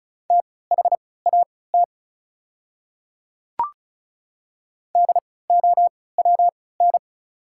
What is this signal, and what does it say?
Morse code, heard as a keyed tone, about 36 wpm, spaced as for 20 wpm, 690 Hz. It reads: THAT DOWN
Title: 3x Character Spacing - Top 100 Words - 35wpm